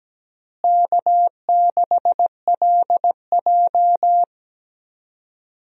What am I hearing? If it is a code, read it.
Morse code, heard as a keyed tone, about 17 wpm, 700 Hz. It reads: K6LJ
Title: USA Call Signs - 4 Characters - Repeated 3x - 17wpm